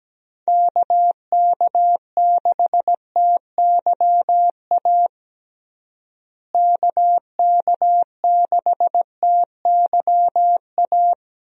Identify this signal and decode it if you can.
Morse code, heard as a keyed tone, about 17 wpm, 700 Hz. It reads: KK6TYA KK6TYA
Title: USA Call Signs - 6 Characters - Repeated 2x - 17wpm